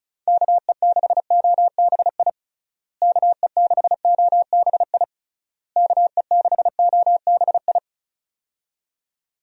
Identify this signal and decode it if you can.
Morse code, heard as a keyed tone, about 34 wpm, 700 Hz. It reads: KE6OBI KE6OBI KE6OBI
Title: USA Call Signs - 6 Characters - Repeated 3x - 35wpm